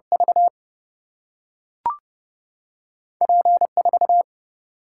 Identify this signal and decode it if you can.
Morse code, heard as a keyed tone, about 30 wpm, 700 Hz. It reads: V P 4
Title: Single-Letter Single-Number 30wpm